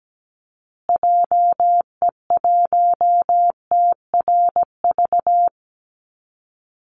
Morse code, heard as a keyed tone, about 17 wpm, 700 Hz. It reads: JE1TRV
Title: Call Signs Repeated 2x - 17wpm